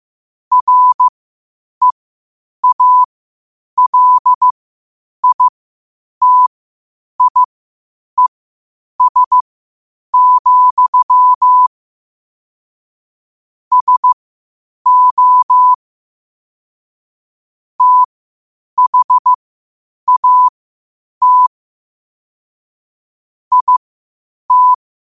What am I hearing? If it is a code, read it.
Morse code, heard as a keyed tone, about 15 wpm, spaced as for 8 wpm, 1000 Hz. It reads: REALITIES, SO THAT IT
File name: qotdfiles / 8wpm